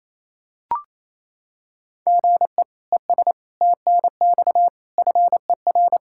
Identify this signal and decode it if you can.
Morse code, heard as a keyed tone, about 28 wpm, 700 Hz. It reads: GE ES TNX FER
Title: Practice QSO - Exchange 1 - Beginning - 28wpm